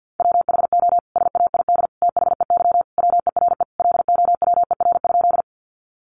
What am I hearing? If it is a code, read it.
Morse code, heard as a keyed tone, about 50 wpm, 700 Hz. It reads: WHO SAID THEY WERE LOWER?
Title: Sentences from Top 700 Words 50wpm